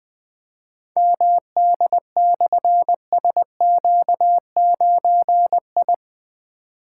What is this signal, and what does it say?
Morse code, heard as a keyed tone, about 20 wpm, 700 Hz. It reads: MD/SQ9I
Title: Difficult Call Signs 20wpm